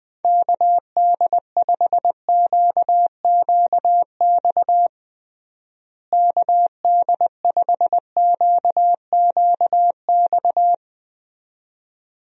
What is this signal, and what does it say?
Morse code, heard as a keyed tone, about 20 wpm, 700 Hz. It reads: KD5QQX KD5QQX
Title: USA Call Signs - 6 Characters - Repeated 2x - 20wpm